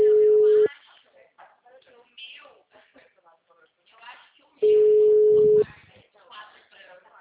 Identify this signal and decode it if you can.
Ringback tone (Europe, 425 Hz), with speech.